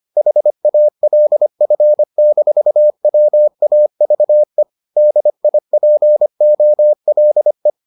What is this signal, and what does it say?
Morse code, heard as a keyed tone, about 25 wpm, 600 Hz. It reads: HALF-WAVE DIPOLE